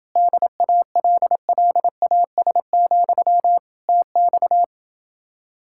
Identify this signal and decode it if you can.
Morse code, heard as a keyed tone, about 29 wpm, 700 Hz. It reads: DALLAS, TX